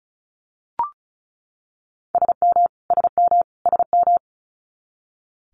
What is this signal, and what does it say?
Morse code, heard as a keyed tone, about 36 wpm, 700 Hz. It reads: SM SM SM
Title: Multi-letter Phonemes Repeated 3x 35wpm